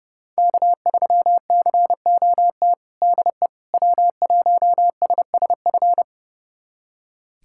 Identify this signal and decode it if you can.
Morse code, heard as a keyed tone, about 30 wpm, 700 Hz. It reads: K3COT DE W1SSF
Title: USA Callsign DE Callsign 30wpm